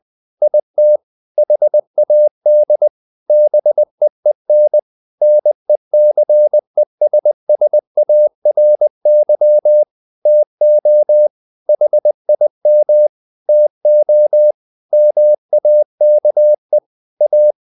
Morse code, heard as a keyed tone, about 20 wpm, 600 Hz. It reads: IT HAD BEEN NECESSARY TO HIM TO MAKE A
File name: the old man and the sea / the old man and the sea part 1 20wpm